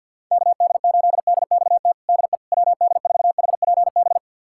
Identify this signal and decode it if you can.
Morse code, heard as a keyed tone, about 48 wpm, 700 Hz. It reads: KD8DXT DE WD4HPB
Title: USA Callsign DE Callsign - 6 Characters - 50wpm